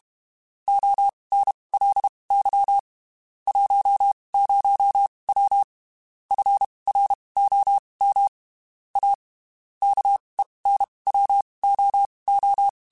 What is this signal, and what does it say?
Morse code, heard as a keyed tone, about 32 wpm, spaced as for 23 wpm, 800 Hz. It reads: ONLY 10W FROM A KENWOO
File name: (spaced) / QSOs Spaced-mp3s / kc7sac @30 sp